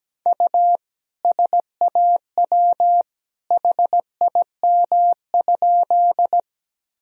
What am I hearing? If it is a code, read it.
Morse code, heard as a keyed tone, about 17 wpm, 700 Hz. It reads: U SAW HIM?
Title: Sentences from Top 300 Words and Abbreviations 17wpm